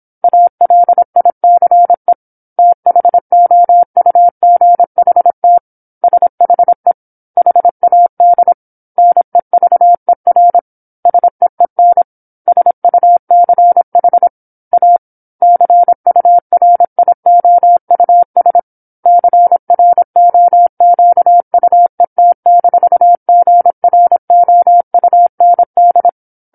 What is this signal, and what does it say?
Morse code, heard as a keyed tone, about 27 wpm, 700 Hz. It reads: ALICE THOUGHT SHE HAD NEVER SEEN SUCH A CURIOUS CROQUET-GROUND
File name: 08-Chapter-08 26 WPM